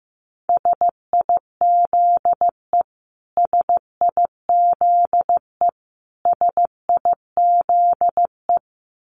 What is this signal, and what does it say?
Morse code, heard as a keyed tone, about 14 wpm, 700 Hz. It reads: SIZE SIZE SIZE
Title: Top 500 Words Repeated 15wpm